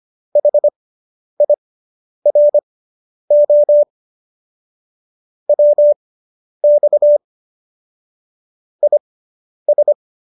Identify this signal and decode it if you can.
Morse code, heard as a keyed tone, about 25 wpm, spaced as for 10 wpm, 590 Hz. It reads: HIRO WX IS